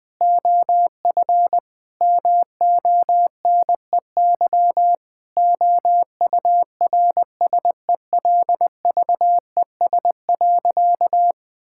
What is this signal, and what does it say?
Morse code, heard as a keyed tone, about 20 wpm, 700 Hz. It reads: OF MONEY OURSELVES.
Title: Sentences from Top 1600 Words 20wpm